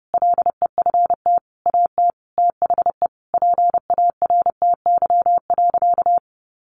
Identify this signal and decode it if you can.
Morse code, heard as a keyed tone, about 30 wpm, 700 Hz. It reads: LEFT AT THE PARTY.
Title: Sentences from Top 1400 Words 30wpm